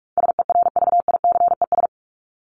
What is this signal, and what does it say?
Morse code, heard as a keyed tone, about 44 wpm, 700 Hz. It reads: SERVICES
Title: ES:S Suffix - Top 100 Words with Base Words 45wpm